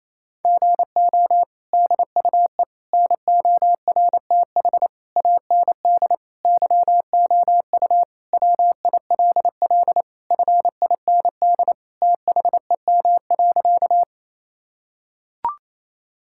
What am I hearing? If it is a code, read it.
Morse code, heard as a keyed tone, about 28 wpm, 700 Hz. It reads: GO DUE NORTH AND YOU WILL FIND THEM.
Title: Sentences from Top 1000 Words - Review Part 6 - 28wpm